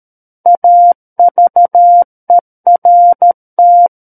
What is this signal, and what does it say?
Morse code, heard as a keyed tone, about 13 wpm, 700 Hz. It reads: AVERT
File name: Book One / 04-Chapter 4 - 13 WPM